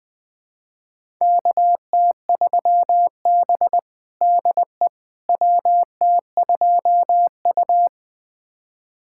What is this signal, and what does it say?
Morse code, heard as a keyed tone, about 21 wpm, 700 Hz. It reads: KT3B DE WT2U
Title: USA Callsign DE Callsign - 4 Characters - 20wpm